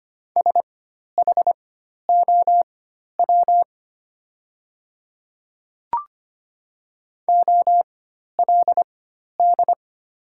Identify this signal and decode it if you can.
Morse code, heard as a keyed tone, about 25 wpm, spaced as for 12 wpm, 700 Hz. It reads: SHOW OLD
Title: 4x Character Spacing - Top 200 Words - 25wpm